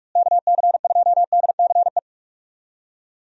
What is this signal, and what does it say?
Morse code, heard as a keyed tone, about 46 wpm, 700 Hz. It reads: KC2DCE
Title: USA Call Signs - 6 Characters - Repeated 3x - 45wpm